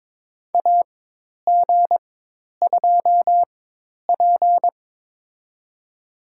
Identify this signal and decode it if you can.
Morse code, heard as a keyed tone, about 22 wpm, spaced as for 10 wpm, 700 Hz. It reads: AG2P
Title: USA Call Signs - 4 Characters - 4x Character Spacing - 22wpm